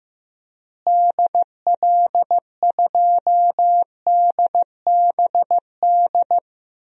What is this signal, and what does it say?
Morse code, heard as a keyed tone, about 15 wpm, 700 Hz. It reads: DL2DBD
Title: DXCC-100 Call Signs - 6 Characters - Repeated 2x - 15wpm